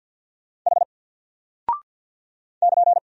Morse code, heard as a keyed tone, about 50 wpm, 700 Hz. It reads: H <KN>
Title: Single Character (Letter:Number:Punctuation:Prosign) Alt 50wpm